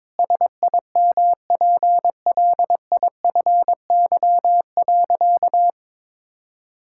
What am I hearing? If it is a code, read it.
Morse code, heard as a keyed tone, about 21 wpm, 700 Hz. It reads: SIMPLIFY.
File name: Sentences with 7-Word Phrases - 22wpm